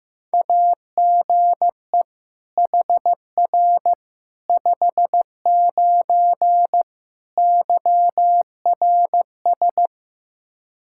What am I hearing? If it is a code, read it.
Morse code, heard as a keyed tone, about 15 wpm, 700 Hz. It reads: AGE HR 59 YRS
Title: K1USN SST QSO Practice 15wpm